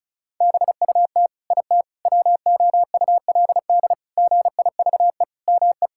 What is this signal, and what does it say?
Morse code, heard as a keyed tone, about 36 wpm, 700 Hz. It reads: BUT IT WOULD GIVE ME
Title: Sentences from Top 1600 Words 35wpm